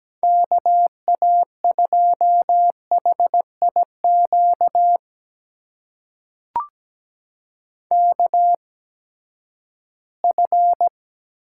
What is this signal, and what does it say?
Morse code, heard as a keyed tone, about 17 wpm, spaced as for 12 wpm, 700 Hz. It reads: KA2HIQ KF
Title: USA Call Signs - 8x Character Spacing - 17wpm